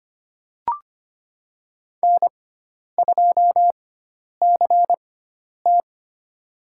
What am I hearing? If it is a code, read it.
Morse code, heard as a keyed tone, about 25 wpm, spaced as for 10 wpm, 700 Hz. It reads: N2CT
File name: Lesson 16g - Letter c - Call Signs 25wpm